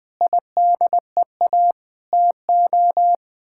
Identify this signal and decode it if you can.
Morse code, heard as a keyed tone, about 20 wpm, 700 Hz. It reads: IDEA TO
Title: Sets of 9 Words - 20wpm